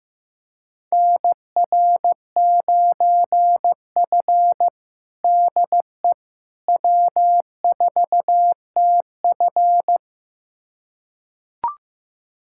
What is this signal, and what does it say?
Morse code, heard as a keyed tone, about 15 wpm, 700 Hz. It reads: NR9F DE W4TF
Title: USA Callsign DE Callsign - 4 Characters - 15wpm